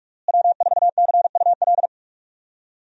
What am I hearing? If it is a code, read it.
Morse code, heard as a keyed tone, about 45 wpm, 700 Hz. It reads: W4CUL
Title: USA Call Signs - 5 Characters - Repeated 2x - 45wpm